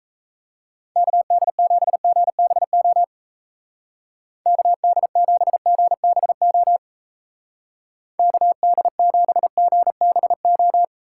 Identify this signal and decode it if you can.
Morse code, heard as a keyed tone, about 34 wpm, 700 Hz. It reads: KD7GBO KD7GBO KD7GBO
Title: Speed Racing - USA 2x3 Call Signs 28wpm